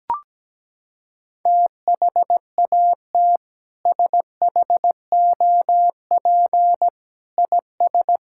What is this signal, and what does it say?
Morse code, heard as a keyed tone, about 17 wpm, 700 Hz. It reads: THAT SHOP IS
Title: Sentences from Top 1600 Words 17wpm